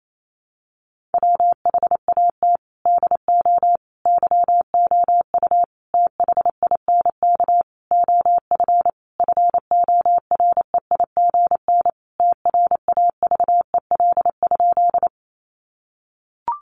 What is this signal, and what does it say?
Morse code, heard as a keyed tone, about 28 wpm, 700 Hz. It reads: WHAT DO YOU THINK OF FOREIGN TRAVEL?
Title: Sentences from Top 1400 Words 28wpm